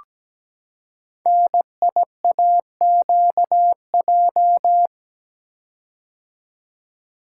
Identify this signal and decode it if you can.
Morse code, heard as a keyed tone, about 17 wpm, 700 Hz. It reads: NIAQJ
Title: Code Groups - 5 Letters - 17wpm